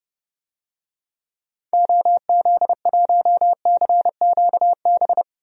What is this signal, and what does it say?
Morse code, heard as a keyed tone, about 30 wpm, 700 Hz. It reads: OZ1CQB
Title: DXCC-100 Call Signs - 6 Characters - Repeated 3x - 30wpm